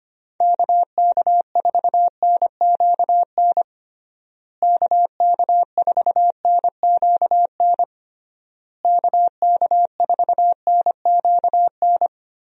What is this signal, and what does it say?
Morse code, heard as a keyed tone, about 25 wpm, 700 Hz. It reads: KK4NQN KK4NQN KK4NQN
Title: USA Call Signs - 6 Characters - Repeated 3x - 25wpm